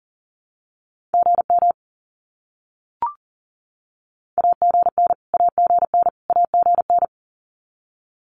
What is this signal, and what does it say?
Morse code, heard as a keyed tone, about 40 wpm, 700 Hz. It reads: GM AGN AGN AGN
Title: Abbreviations Repeated 3x - 40wpm